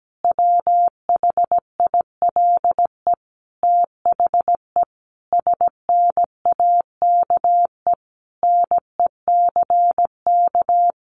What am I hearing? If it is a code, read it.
Morse code, heard as a keyed tone, about 17 wpm, 700 Hz. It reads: WHILE THE SNAKE NECK
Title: A Sound Of Thunder 17wpm